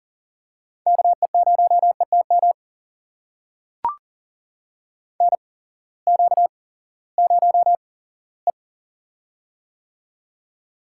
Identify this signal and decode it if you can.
Morse code, heard as a keyed tone, about 40 wpm, spaced as for 20 wpm, 700 Hz. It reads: KE0ETM NQ0E
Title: USA Call Signs - 8x Character Spacing - 40wpm